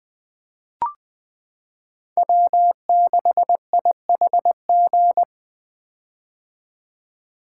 Morse code, heard as a keyed tone, about 20 wpm, 700 Hz. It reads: W6IHG
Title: K1USN SST QSO Practice - 20wpm